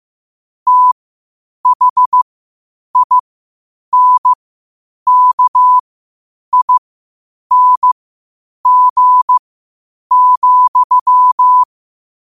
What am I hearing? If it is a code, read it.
Morse code, heard as a keyed tone, about 15 wpm, spaced as for 8 wpm, 1000 Hz. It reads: THINKING,
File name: qotdfiles / 8wpm